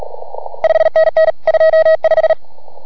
Morse code, heard as a keyed tone, about 32 wpm, 690 Hz. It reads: 5NN 25